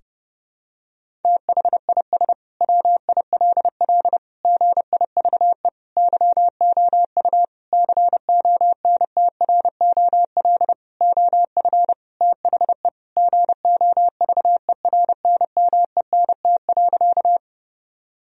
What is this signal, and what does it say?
Morse code, heard as a keyed tone, about 30 wpm, 700 Hz. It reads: THIS WILL GIVE YOU CONTROL OF THE GOVERNMENT.
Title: Sentences from Top 1000 Words - Review Part 3 - 30wpm